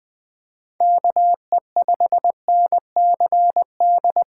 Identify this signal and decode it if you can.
Morse code, heard as a keyed tone, about 20 wpm, 700 Hz. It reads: KE5NCD
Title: USA Call Signs - 6 Characters - 3x Character Spacing - 20wpm